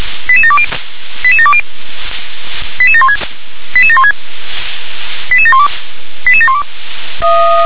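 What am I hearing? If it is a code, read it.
Five-tone selective call (ZVEI), heard as quick runs of five tones, tones of about 70 ms. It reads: 88720 88720 88716 88716 88712 88712
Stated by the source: ZVEI five-tone call 88730 received by SDR.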